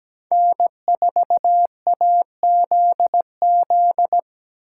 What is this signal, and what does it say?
Morse code, heard as a keyed tone, about 20 wpm, 700 Hz. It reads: N4AZZ
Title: Speed Racing - USA 1x3 Call Signs 17wpm